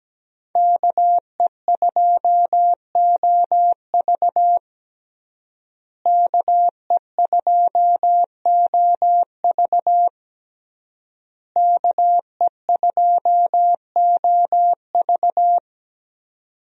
Morse code, heard as a keyed tone, about 17 wpm, 700 Hz. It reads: KE2OV KE2OV KE2OV
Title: USA Call Signs - 5 Characters - Repeated 3x - 17wpm